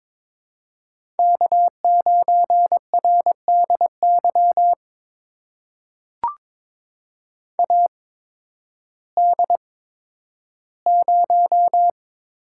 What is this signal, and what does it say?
Morse code, heard as a keyed tone, about 22 wpm, spaced as for 12 wpm, 700 Hz. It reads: K9RDY AD0
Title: USA Call Signs - 8x Character Spacing - 22wpm